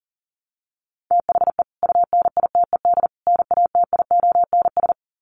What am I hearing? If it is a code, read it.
Morse code, heard as a keyed tone, about 40 wpm, 700 Hz. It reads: THE UNITED NATIONS
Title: Sets of 3 Words 40wpm